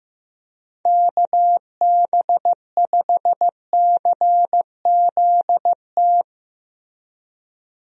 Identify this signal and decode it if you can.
Morse code, heard as a keyed tone, about 15 wpm, 700 Hz. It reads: KB5CZT
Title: USA Call Signs - 6 Characters - Repeated 3x - 15wpm